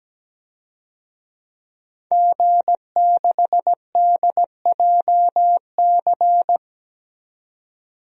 Morse code, heard as a keyed tone, about 17 wpm, 700 Hz. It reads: G6DJC
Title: DXCC-100 Call Signs - 5 Characters - 17wpm